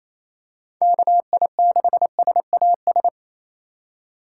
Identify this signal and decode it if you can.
Morse code, heard as a keyed tone, about 28 wpm, 700 Hz. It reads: KI6SAS
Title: USA Call Signs - 2x Character Spacing - 28wpm